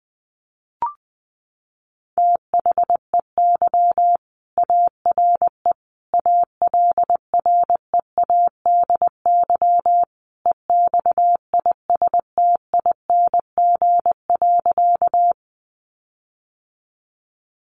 Morse code, heard as a keyed tone, about 20 wpm, 700 Hz. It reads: THEY ARE ALREADY EXISTING.
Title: Sentences from Top 1500 Words 20wpm